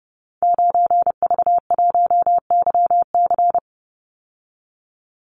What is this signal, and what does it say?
Morse code, heard as a keyed tone, about 30 wpm, 700 Hz. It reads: 9V1YC
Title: Call Signs - 30wpm